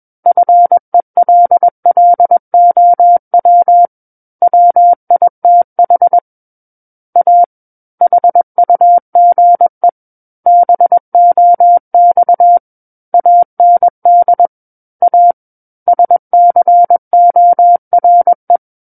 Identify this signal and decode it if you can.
Morse code, heard as a keyed tone, about 21 wpm, 700 Hz. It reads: FELLOW WITH A HUGE BOX AND A SCORE
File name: Book One / 12-Chapter 12 - 21 WPM